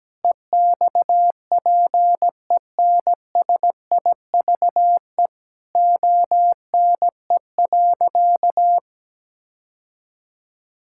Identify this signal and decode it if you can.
Morse code, heard as a keyed tone, about 17 wpm, 700 Hz. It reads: EXPENSIVE ONE.
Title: Sentences from Top 1000 Words - Review Part 6 - 17wpm